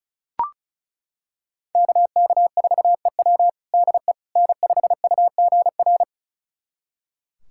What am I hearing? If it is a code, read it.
Morse code, heard as a keyed tone, about 36 wpm, 700 Hz. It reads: KK4EW DE N5UGR
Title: USA Callsign DE Callsign - 5 Characters - 35wpm